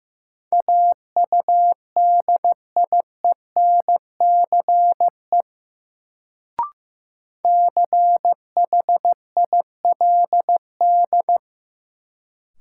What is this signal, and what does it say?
Morse code, heard as a keyed tone, about 16 wpm, 700 Hz. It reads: AUDIENCE CHILD
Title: Sentences from Top 800 Words - New Word Drill - 15wpm